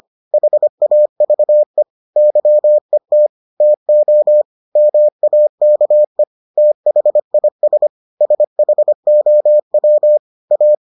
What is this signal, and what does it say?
Morse code, heard as a keyed tone, about 25 wpm, 600 Hz. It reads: HAVE YET TO MAKE THIS SHOW A